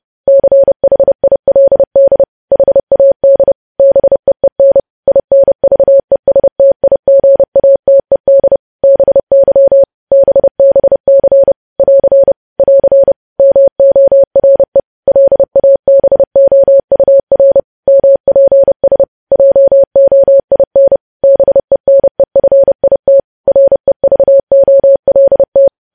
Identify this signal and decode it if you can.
Morse code, heard as a keyed tone, about 30 wpm, 550 Hz. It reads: CHILD HAD BEEN INVESTIGATED BY BBC <AR> <AR> MORE LABOUR MPS JOIN BENEFIT REVOLT
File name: podcast / News headlines-1x-30x30